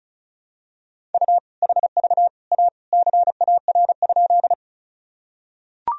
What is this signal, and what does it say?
Morse code, heard as a keyed tone, about 36 wpm, 700 Hz. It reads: U HV A CAR?
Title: Sentences from Top 300 Words and Abbreviations 35wpm